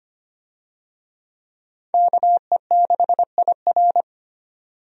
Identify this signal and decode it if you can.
Morse code, heard as a keyed tone, about 25 wpm, 700 Hz. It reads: KE6IR
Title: USA Call Signs - 5 Characters - 25wpm